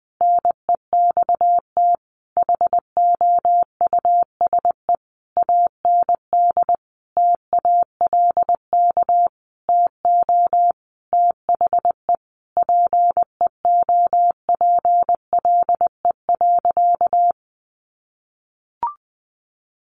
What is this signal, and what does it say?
Morse code, heard as a keyed tone, about 20 wpm, 700 Hz. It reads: NEXT HOUSE AND TALK TO THE PEOPLE.
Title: Sentences from Top 400 Words 20wpm